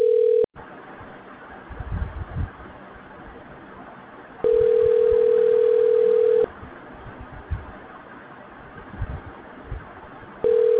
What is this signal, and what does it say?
Ringback tone (North America), 440 + 480 Hz.